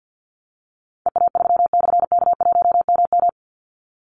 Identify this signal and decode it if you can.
Morse code, heard as a keyed tone, about 50 wpm, 700 Hz. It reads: EA3/K1MM
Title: Call Signs - 50wpm